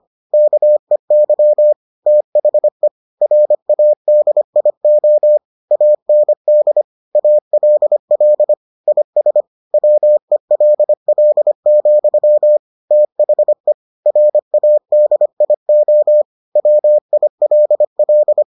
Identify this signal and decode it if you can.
Morse code, heard as a keyed tone, about 25 wpm, 600 Hz. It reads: KEY THE RADIO AND ALL IS WELL, THE RADIO WILL